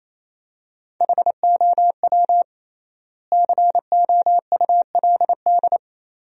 Morse code, heard as a keyed tone, about 28 wpm, 700 Hz. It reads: HOW COULD
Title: 3x Word-Spacing - Sentences from Top 400 Words - 28wpm